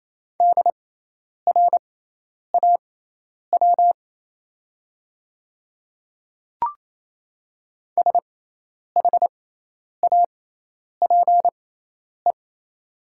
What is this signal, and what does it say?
Morse code, heard as a keyed tone, about 27 wpm, spaced as for 10 wpm, 700 Hz. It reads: DRAW SHAPE
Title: 6x Character Spacing - Top 500 Words - 28wpm